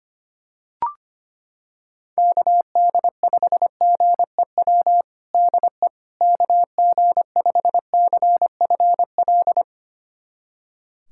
Morse code, heard as a keyed tone, about 25 wpm, 700 Hz. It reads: KD5GEW DE KG5CFL
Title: USA Callsign DE Callsign - 6 Characters - 25wpm